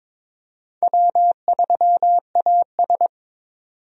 Morse code, heard as a keyed tone, about 22 wpm, 700 Hz. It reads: W3AS
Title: USA Call Signs - 4 Characters - 22wpm